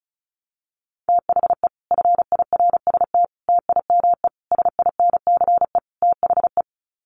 Morse code, heard as a keyed tone, about 34 wpm, 700 Hz. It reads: THE FIRST TIME SINCE THE
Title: Sets of 5 Words Encore 35wpm